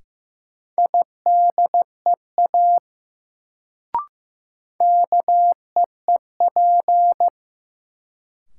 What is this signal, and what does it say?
Morse code, heard as a keyed tone, about 15 wpm, 700 Hz. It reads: IDEA KEEP
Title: Sentences from Top 300 Words - New Word Drill - 15wpm